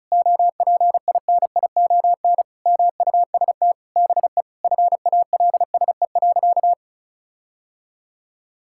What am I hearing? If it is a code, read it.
Morse code, heard as a keyed tone, about 36 wpm, 700 Hz. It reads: OPINION MUST BE FALSE.
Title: Sentences from Top 1600 Words 35wpm